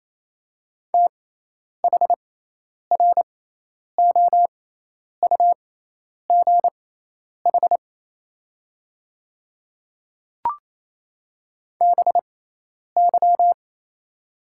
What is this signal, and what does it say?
Morse code, heard as a keyed tone, about 28 wpm, spaced as for 10 wpm, 700 Hz. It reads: THROUGH BY
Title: 6x Character Spacing - Top 200 Words - 28wpm